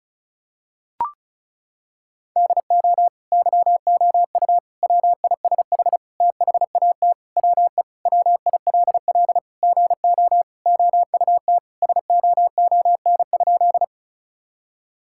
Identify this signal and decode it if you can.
Morse code, heard as a keyed tone, about 34 wpm, 700 Hz. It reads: DO YOU WISH THAT WE WILL GO OUT SOON?
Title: Sentences from Top 1600 Words 35wpm